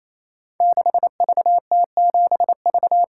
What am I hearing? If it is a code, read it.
Morse code, heard as a keyed tone, about 27 wpm, 700 Hz. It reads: 6VT7V
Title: Code Groups - 5 Letter:Numbers - 28wpm